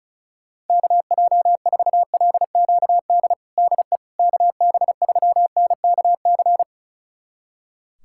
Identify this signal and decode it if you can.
Morse code, heard as a keyed tone, about 36 wpm, 700 Hz. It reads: KJ4LQD DE KB3NKC
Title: USA Callsign DE Callsign - 6 Characters - 35wpm